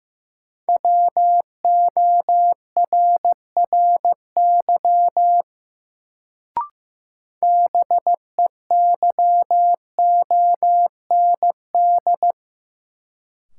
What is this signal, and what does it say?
Morse code, heard as a keyed tone, about 15 wpm, 700 Hz. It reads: WORRY BEYOND
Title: Sentences from Top 1100 Words - New Word Drill Alt - 15wpm